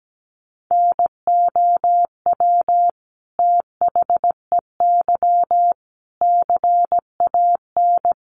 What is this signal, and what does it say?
Morse code, heard as a keyed tone, about 17 wpm, 700 Hz. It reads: NOW THEY CAN
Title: Sentences from Top 700 Words 17wpm